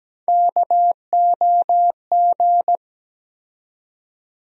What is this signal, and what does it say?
Morse code, heard as a keyed tone, about 17 wpm, 700 Hz. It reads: KOG
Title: USA Call Sign Postfixes - 3 Letters 17wpm